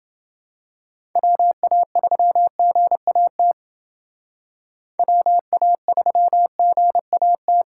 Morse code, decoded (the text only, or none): WA3GAT WA3GAT